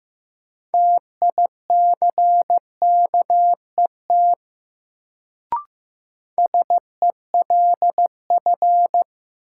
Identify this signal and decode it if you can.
Morse code, heard as a keyed tone, about 15 wpm, 700 Hz. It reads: TICKET SELF
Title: Sentences from Top 1400 Words - New Word Drill Alt - 15wpm